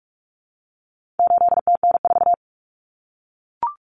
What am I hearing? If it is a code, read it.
Morse code, heard as a keyed tone, about 44 wpm, 700 Hz. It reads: 8TN4
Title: Common Cut Numbers - 4-Digit 45wpm